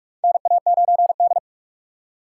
Morse code, heard as a keyed tone, about 46 wpm, 700 Hz. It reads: NA9D
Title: USA Call Signs - 4 Characters - 4x Character Spacing - 45wpm